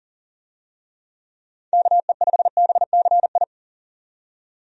Morse code, heard as a keyed tone, about 40 wpm, 700 Hz. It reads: KE5BCI
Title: USA Call Signs - 6 Characters - 40wpm